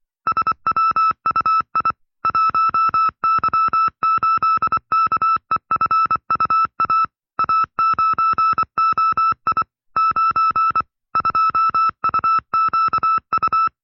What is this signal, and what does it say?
Morse code, heard as a keyed tone, about 24 wpm, 1310 Hz. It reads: SWUI 1Y8KEFUA A9OI 9 2UQU
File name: FreeFiles / CW-practice / 20WPM-side2